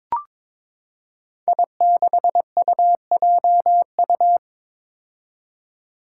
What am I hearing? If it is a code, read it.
Morse code, heard as a keyed tone, about 22 wpm, 700 Hz. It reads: I6UJU
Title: Five Challenging Letter-Number Combinations 22wpm